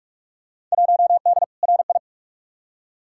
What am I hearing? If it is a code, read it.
Morse code, heard as a keyed tone, about 47 wpm, 700 Hz. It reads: 1D RI
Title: Field Day Exchanges - 45wpm